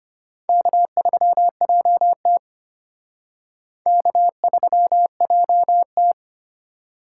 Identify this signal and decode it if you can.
Morse code, heard as a keyed tone, about 32 wpm, 700 Hz. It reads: K3JT K3JT